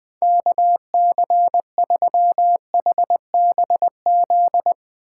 Morse code, heard as a keyed tone, about 20 wpm, 700 Hz. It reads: KC3HBZ